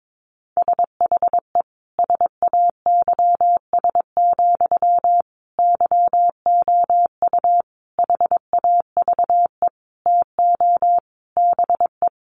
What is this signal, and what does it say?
Morse code, heard as a keyed tone, about 22 wpm, 700 Hz. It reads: SHE SAYS, YOU HAVE TO BE
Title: Easy Phrases 22wpm